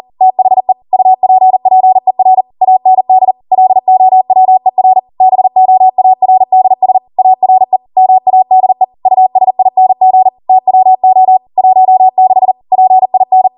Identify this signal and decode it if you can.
Morse code, heard as a keyed tone, about 40 wpm, 750 Hz. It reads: THE UPPER AND LOWER BOARDS ARE MADE USING TWO 16 PIN
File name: Morse / 40WPM